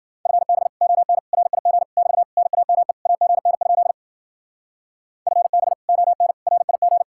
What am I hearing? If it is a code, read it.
Morse code, heard as a keyed tone, about 49 wpm, 700 Hz. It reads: FB ON RIG <BT> NAME AGN? FB ON RIG
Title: Speed-Racing Common QSO Phrases Encore - 40wpm